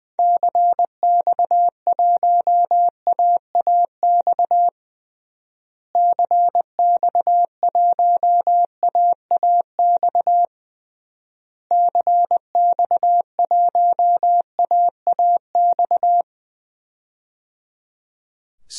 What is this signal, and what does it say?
Morse code, heard as a keyed tone, about 20 wpm, 700 Hz. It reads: CX1AAX CX1AAX CX1AAX
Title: DXCC-100 Call Signs - 6 Characters - Repeated 3x - 20wpm